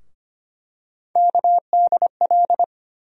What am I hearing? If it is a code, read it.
Morse code, heard as a keyed tone, about 25 wpm, 700 Hz. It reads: KDL
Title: USA Call Sign Postfixes - 3 Letters 25wpm